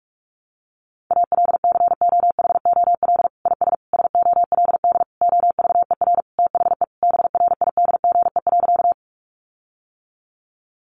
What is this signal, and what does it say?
Morse code, heard as a keyed tone, about 44 wpm, 700 Hz. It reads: ALCOHOL IS SOLD OVER THE BRIDGE.
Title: Sentences from Top 1400 Words 45wpm